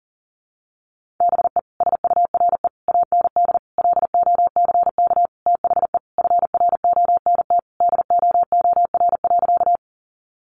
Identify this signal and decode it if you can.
Morse code, heard as a keyed tone, about 40 wpm, 700 Hz. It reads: BE SURE AND LOCK THE FRONT DOOR.
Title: Sentences from Top 1300 Words 40wpm